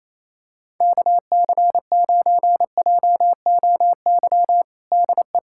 Morse code, heard as a keyed tone, about 27 wpm, 700 Hz. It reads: KC9JOY DE
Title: USA Callsign DE Callsign - 6 Characters - 28wpm